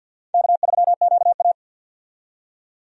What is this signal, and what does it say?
Morse code, heard as a keyed tone, about 50 wpm, 700 Hz. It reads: K3QA
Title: USA Call Signs - 4 Characters - 50wpm